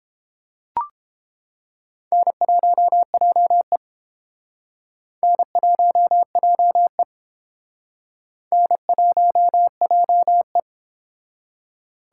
Morse code, heard as a keyed tone, about 30 wpm, 700 Hz. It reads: N1JE N1JE N1JE